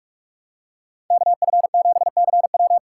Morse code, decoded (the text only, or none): KF7CW